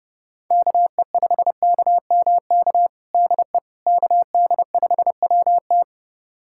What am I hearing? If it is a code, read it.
Morse code, heard as a keyed tone, about 30 wpm, 700 Hz. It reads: KE5KMK DE KD5WT
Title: USA Callsign DE Callsign 30wpm